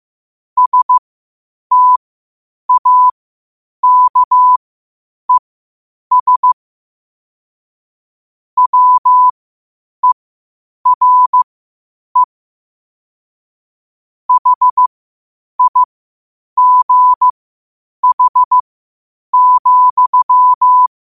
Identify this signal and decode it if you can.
Morse code, heard as a keyed tone, about 15 wpm, spaced as for 8 wpm, 1000 Hz. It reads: STAKES WERE HIGH,
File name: qotdfiles / 8wpm